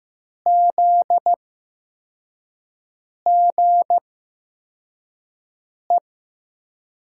Morse code, heard as a keyed tone, about 15 wpm, spaced as for 4 wpm, 700 Hz. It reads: ZGE
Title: USA Call Signs - 8x Character Spacing - 15wpm